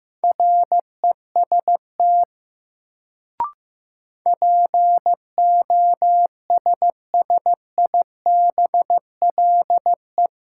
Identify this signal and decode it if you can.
Morse code, heard as a keyed tone, about 15 wpm, 700 Hz. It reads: REST POSSIBLE
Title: Sentences from Top 500 Words - New Word Drill - 15wpm